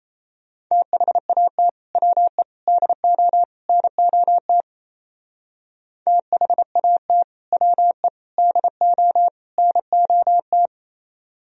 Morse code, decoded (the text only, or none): THAT WE DO NOT THAT WE DO NOT